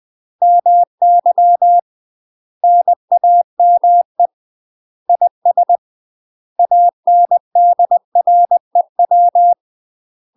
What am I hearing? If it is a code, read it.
Morse code, heard as a keyed tone, about 20 wpm, 700 Hz. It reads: MY NAME IS ANDREW
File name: UK MASTER QSOs TRAINING / 20WPM / UK MASTER QSOs 0014